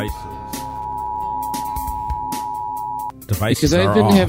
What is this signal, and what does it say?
Emergency Alert System (SAME) tones (the 853 and 960 Hz attention tone), with speech. It is heard from 0.6 to 3.2 s.